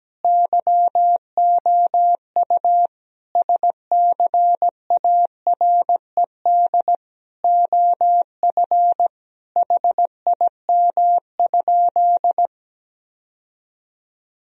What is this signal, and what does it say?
Morse code, heard as a keyed tone, about 17 wpm, 700 Hz. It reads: YOU SCARED OF HIM?
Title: Sentences from Top 1000 Words - Review Part 4 - 17wpm